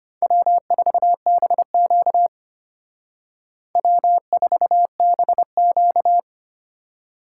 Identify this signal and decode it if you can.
Morse code, heard as a keyed tone, about 32 wpm, 700 Hz. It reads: W4BQ W4BQ